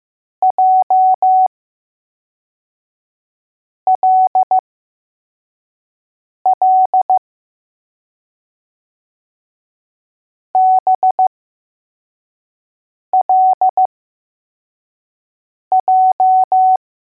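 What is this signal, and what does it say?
Morse code, heard as a keyed tone, about 15 wpm, spaced as for 4 wpm, 750 Hz. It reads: JLL BLJ
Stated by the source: Morse code letters test drill for L, J and B.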